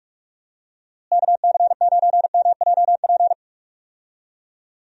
Morse code, heard as a keyed tone, about 42 wpm, 700 Hz. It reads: KC9MJP
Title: USA Call Signs - 6 Characters - Repeated 2x - 45wpm